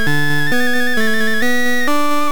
There is no code.